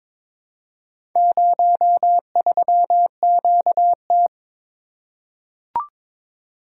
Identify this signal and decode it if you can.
Morse code, heard as a keyed tone, about 23 wpm, 700 Hz. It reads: 03QT
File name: practice / 22wpm / Four Letter-Number Combinations - 22wpm